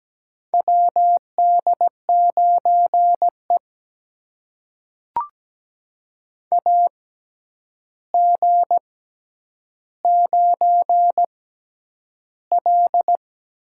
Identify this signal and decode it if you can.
Morse code, heard as a keyed tone, about 17 wpm, spaced as for 9 wpm, 700 Hz. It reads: WD9E AG9L
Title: USA Call Signs - 4 Characters - 6x Character Spacing - 17wpm